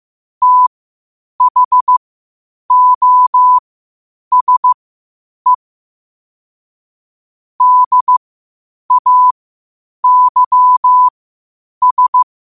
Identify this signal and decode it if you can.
Morse code, heard as a keyed tone, about 15 wpm, spaced as for 8 wpm, 1000 Hz. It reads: THOSE DAYS
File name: qotdfiles / 8wpm